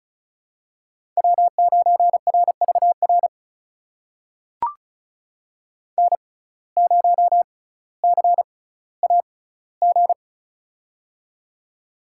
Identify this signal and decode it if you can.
Morse code, heard as a keyed tone, about 36 wpm, spaced as for 18 wpm, 700 Hz. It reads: W9RVR N0CAG
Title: USA Call Signs - 6x Character Spacing - 35wpm